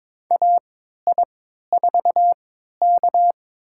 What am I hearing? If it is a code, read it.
Morse code, heard as a keyed tone, about 22 wpm, spaced as for 13 wpm, 700 Hz. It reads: AI4K
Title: USA Call Signs - 3x Character Spacing - 22wpm